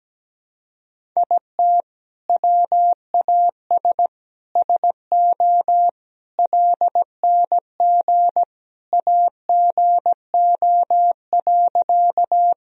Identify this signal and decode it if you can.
Morse code, heard as a keyed tone, about 17 wpm, 700 Hz. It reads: IT WAS SO LNG AGO.
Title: Sentences from Top 500 Words and Abbreviations 17wpm